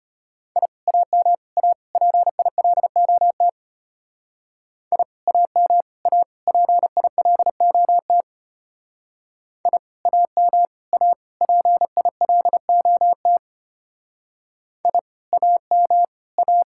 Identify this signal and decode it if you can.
Morse code, heard as a keyed tone, about 32 wpm, 700 Hz. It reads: I AM A PILOT I AM A PILOT I AM A PILOT I AM A